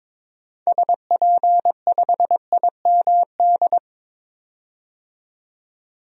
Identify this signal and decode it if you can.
Morse code, heard as a keyed tone, about 21 wpm, 700 Hz. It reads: SP5IMD
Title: DXCC-100 Call Signs - 6 Characters - 22wpm